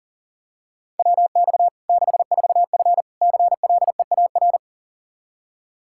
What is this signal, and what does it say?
Morse code, heard as a keyed tone, about 40 wpm, 700 Hz. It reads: WX 64F CLEAR
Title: Practice QSO - Exchange 2 - Info - 40wpm